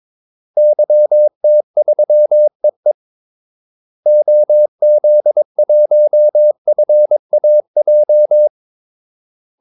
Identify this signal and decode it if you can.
Morse code, heard as a keyed tone, about 22 wpm, 600 Hz. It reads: YT3EE OZ1FAJ
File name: Callsigns-22wpm